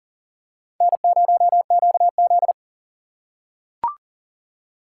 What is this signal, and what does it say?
Morse code, heard as a keyed tone, about 40 wpm, 700 Hz. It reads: N0QZ
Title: USA Call Signs - 4 Characters - 4x Character Spacing - 40wpm